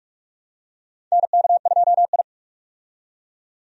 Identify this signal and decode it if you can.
Morse code, heard as a keyed tone, about 41 wpm, 700 Hz. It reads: NK2I
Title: USA Call Signs - 4 Characters - 45wpm